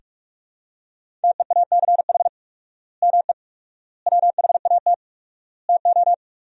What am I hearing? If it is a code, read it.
Morse code, heard as a keyed tone, about 42 wpm, 690 Hz. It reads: TEACH ME WHAT TO
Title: 4x Word-Spacing - Sentences from Top 500 Words - 45wpm